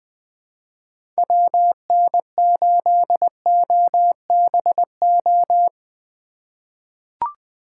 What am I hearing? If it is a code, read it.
Morse code, heard as a keyed tone, about 20 wpm, 700 Hz. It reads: WN8OBO
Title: USA Call Signs - 6 Characters - 4x Character Spacing - 20wpm